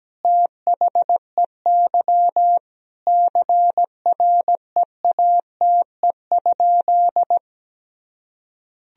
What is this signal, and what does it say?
Morse code, heard as a keyed tone, about 17 wpm, 700 Hz. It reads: THEY CREATE?
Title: Sentences from Top 1000 Words - Review Part 3 - 17wpm